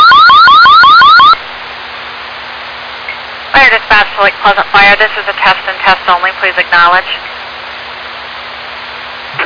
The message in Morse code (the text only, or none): none